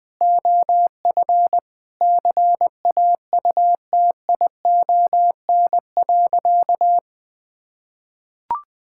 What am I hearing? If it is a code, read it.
Morse code, heard as a keyed tone, about 20 wpm, 700 Hz. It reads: OF CAUTION.
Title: Sentences with 7-Word Phrases - 20wpm